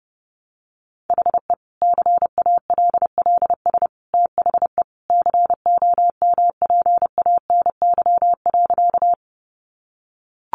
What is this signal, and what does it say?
Morse code, heard as a keyed tone, about 30 wpm, 700 Hz. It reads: HE CALLS THE COMPANY.
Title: Additional Easy Phrases 30wpm